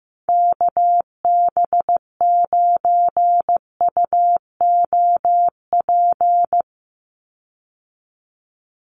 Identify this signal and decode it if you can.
Morse code, heard as a keyed tone, about 15 wpm, 700 Hz. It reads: KB9UOP
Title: USA Call Signs 15wpm